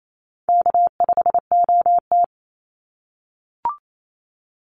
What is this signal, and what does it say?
Morse code, heard as a keyed tone, about 28 wpm, 700 Hz. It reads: K5OT
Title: NA SOTA Chaser Callsigns 28wpm